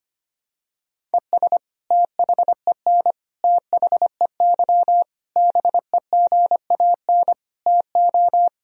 Morse code, heard as a keyed tone, about 25 wpm, 700 Hz. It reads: ES THEN THEY BEGAN TO
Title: Sentences from Top 300 Words and Abbreviations 25wpm